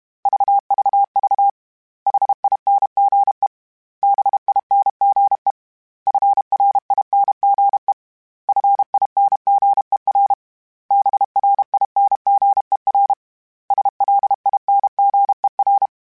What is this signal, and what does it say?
Morse code, heard as a keyed tone, about 32 wpm, 800 Hz. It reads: VVV HINGE BINGE FRINGE FINGER BRINGER SLINGER